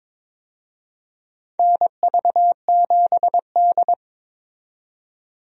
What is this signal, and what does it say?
Morse code, heard as a keyed tone, about 22 wpm, 700 Hz. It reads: NV7D
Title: USA Call Signs - 4 Characters - 22wpm